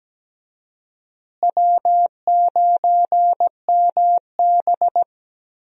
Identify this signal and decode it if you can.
Morse code, heard as a keyed tone, about 17 wpm, 700 Hz. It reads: W9MB
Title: USA Call Signs - 4 Characters - Repeated 3x - 17wpm